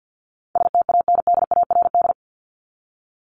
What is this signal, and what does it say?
Morse code, heard as a keyed tone, about 50 wpm, 700 Hz. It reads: STANDARD
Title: Top 751-1000 Words 50wpm